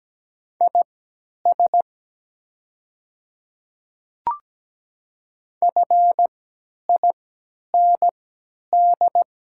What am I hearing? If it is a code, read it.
Morse code, heard as a keyed tone, about 17 wpm, spaced as for 10 wpm, 700 Hz. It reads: IS FIND
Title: 3x Character Spacing - Top 100 Words - 17wpm